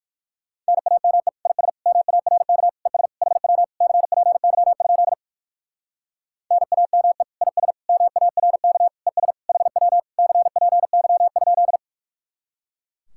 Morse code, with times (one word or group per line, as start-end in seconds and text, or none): 0.7-1.3 NAME
1.5-1.7 IS
1.9-2.7 MARK
2.9-3.1 ES
3.2-3.6 HW
3.8-5.1 CPY?
6.5-7.2 NAME
7.4-7.7 IS
7.9-8.9 MARK
9.1-9.3 ES
9.5-10.0 HW
10.2-11.8 CPY?